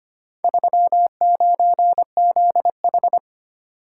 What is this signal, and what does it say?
Morse code, heard as a keyed tone, about 25 wpm, 700 Hz. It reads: 39ZH
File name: practice / 25wpm / Four Letter-Number Combinations - 25wpm